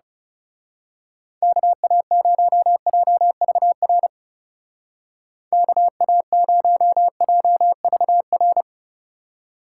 Morse code, heard as a keyed tone, about 34 wpm, 700 Hz. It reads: KA0JVR KA0JVR
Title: Speed Racing - USA 2x3 Call Signs 30wpm